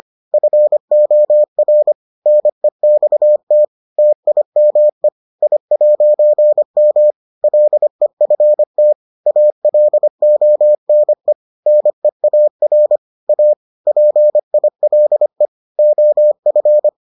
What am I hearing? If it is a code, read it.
Morse code, heard as a keyed tone, about 25 wpm, 600 Hz. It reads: FOR NEXT TIME I'M LEFT ALONE NEAR A PILE OF